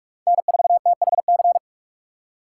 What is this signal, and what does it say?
Morse code, heard as a keyed tone, about 46 wpm, 700 Hz. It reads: N4THC
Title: USA Call Signs - 5 Characters - 4x Character Spacing - 45wpm